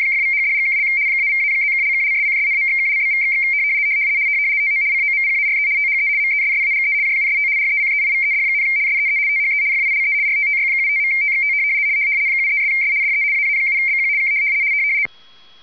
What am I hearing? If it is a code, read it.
RTTY, about 75 baud. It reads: !=#$&() -=-+()()':',./()?=// TO COMPARE THE MESSAGE JUST RECEIVED, SEE THE FILE RTTY75.TXT 73'S DE AA7BQ K